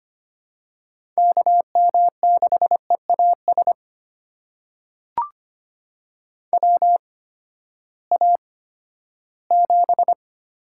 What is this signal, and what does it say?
Morse code, heard as a keyed tone, about 25 wpm, spaced as for 14 wpm, 700 Hz. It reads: KM6EAS WA7
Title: USA Call Signs - 6 Characters - 8x Character Spacing - 25wpm